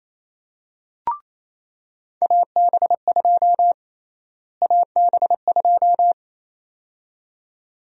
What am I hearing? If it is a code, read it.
Morse code, heard as a keyed tone, about 28 wpm, 700 Hz. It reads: AB2 AB2
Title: USA Call Sign Prefixes - Repeated 2x - 28wpm